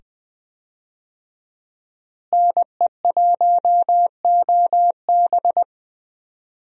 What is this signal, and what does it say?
Morse code, heard as a keyed tone, about 20 wpm, 700 Hz. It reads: NE1OB
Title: USA Call Signs - 5 Characters - Repeated 2x - 20wpm